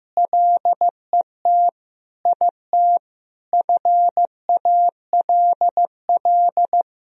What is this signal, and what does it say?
Morse code, heard as a keyed tone, about 15 wpm, 700 Hz. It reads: LET IT FALL
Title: Sentences from Top 1000 Words - Review Part 2 - 15wpm